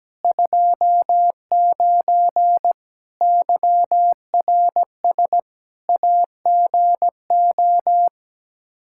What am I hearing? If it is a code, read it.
Morse code, heard as a keyed tone, about 17 wpm, 700 Hz. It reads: 29 YRS AGO
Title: Practice QSO - Exchange 3 - More Info - 17wpm